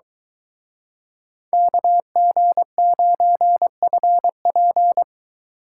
Morse code, heard as a keyed tone, about 24 wpm, 700 Hz. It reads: KG9FP